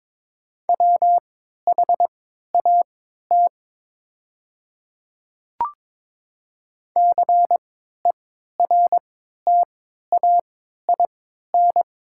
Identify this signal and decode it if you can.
Morse code, heard as a keyed tone, about 22 wpm, spaced as for 12 wpm, 700 Hz. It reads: WHAT CERTAIN
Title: 3x Character Spacing - Top 500 Words - 22wpm